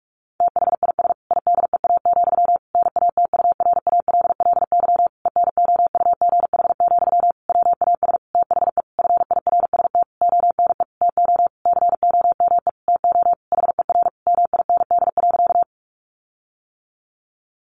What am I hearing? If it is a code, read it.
Morse code, heard as a keyed tone, about 44 wpm, 700 Hz. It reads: THIS IDEA, NATURALLY ENOUGH, WAS THE FIRST ONE TO COME TO HER MIND.
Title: Sentences from Top 1300 Words 45wpm